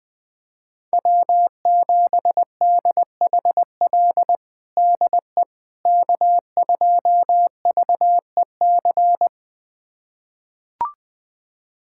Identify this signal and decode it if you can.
Morse code, heard as a keyed tone, about 20 wpm, 700 Hz. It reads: W7DHL DE K2VEC
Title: USA Callsign DE Callsign - 5 Characters - 20wpm